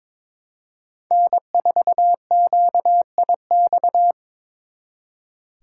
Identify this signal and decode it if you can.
Morse code, heard as a keyed tone, about 26 wpm, 700 Hz. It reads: N4QIX
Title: Speed Racing - USA 1x3 Call Signs 22wpm